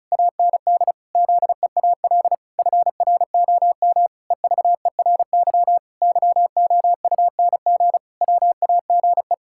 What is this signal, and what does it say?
Morse code, heard as a keyed tone, about 36 wpm, 700 Hz. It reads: AND ZEAL FROM EVERY YOUNG WAGE
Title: Pangram Challenge v2 - 35wpm